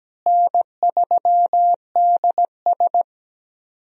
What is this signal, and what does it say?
Morse code, heard as a keyed tone, about 17 wpm, 700 Hz. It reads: N3DS
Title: USA Call Signs - 4 Characters - Repeated 2x - 17wpm